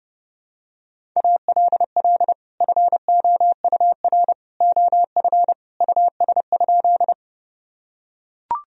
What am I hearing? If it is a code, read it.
Morse code, heard as a keyed tone, about 30 wpm, 700 Hz. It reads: ALL FOUR OF US?
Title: Sentences from Top 1000 Words - Review Part 1 - 30wpm